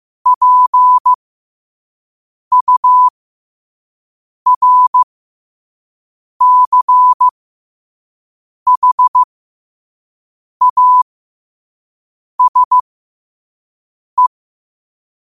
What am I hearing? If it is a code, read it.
Morse code, heard as a keyed tone, about 15 wpm, spaced as for 5 wpm, 1000 Hz. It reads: PURCHASE
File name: qotdfiles / 5wpm